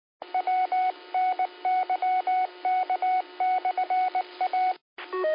Morse code, heard as a keyed tone, about 19 wpm, 740 Hz, with radio noise. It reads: WNYK/A
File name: qsorec SimplexLogic